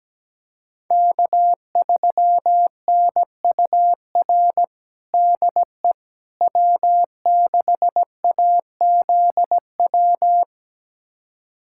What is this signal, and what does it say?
Morse code, heard as a keyed tone, about 17 wpm, 700 Hz. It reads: K3NUR DE W6AZW
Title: USA Callsign DE Callsign - 5 Characters - 17wpm